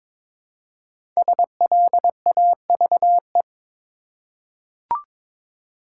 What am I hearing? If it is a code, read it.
Morse code, heard as a keyed tone, about 23 wpm, 700 Hz. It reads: SLAVE
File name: 5 Letter Words - First 3 Letters Given - 22wpm